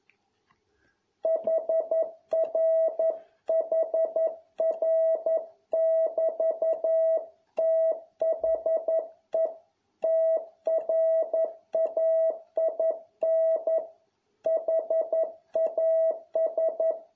Morse code, heard as a keyed tone, about 11 wpm, 650 Hz. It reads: HRHR<BT>THE TRAIN HAS